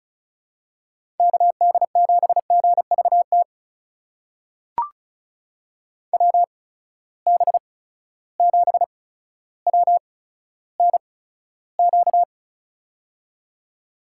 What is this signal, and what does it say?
Morse code, heard as a keyed tone, about 36 wpm, spaced as for 15 wpm, 700 Hz. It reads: KD7GVT WB7WNQ
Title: USA Call Signs - 6 Characters - 8x Character Spacing - 35wpm